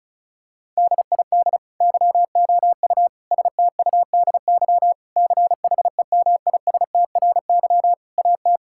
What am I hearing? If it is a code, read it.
Morse code, heard as a keyed tone, about 36 wpm, 700 Hz. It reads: DID YOU STUDY CHEMISTRY AT
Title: Sentences from Top 1000 Words - Review Part 6 - 35wpm